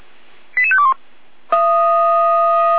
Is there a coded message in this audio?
Yes: five-tone selective call (ZVEI) from 0.6 to 0.9 s, heard as quick runs of five tones, tones of about 70 ms. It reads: 88521